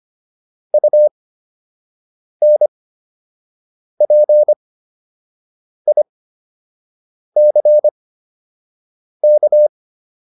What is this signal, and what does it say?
Morse code, heard as a keyed tone, about 25 wpm, spaced as for 6 wpm, 600 Hz. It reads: UNPICK